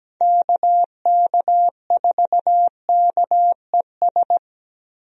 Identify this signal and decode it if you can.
Morse code, heard as a keyed tone, about 17 wpm, 700 Hz. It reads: KK4KES
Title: USA Call Signs - 6 Characters - Repeated 2x - 17wpm